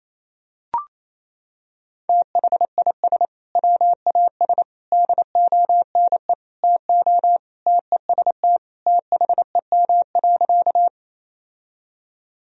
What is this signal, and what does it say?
Morse code, heard as a keyed tone, about 28 wpm, 700 Hz. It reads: THIS WAS DONE TO TEST THEM.
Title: Sentences from Top 1000 Words - Review Part 3 - 28wpm